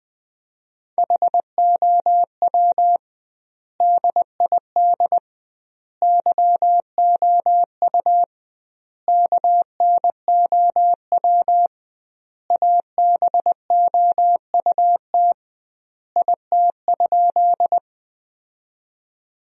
Morse code, heard as a keyed tone, about 20 wpm, 700 Hz. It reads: HOW DID YOU KNOW ABOUT IT?
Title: 2x Word-Spacing - Sentences from Top 100 Words - 20wpm